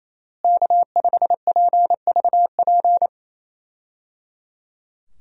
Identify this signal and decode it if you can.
Morse code, heard as a keyed tone, about 28 wpm, 700 Hz. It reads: K5PVP
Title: USA Call Signs - 5 Characters - Repeated 3x - 28wpm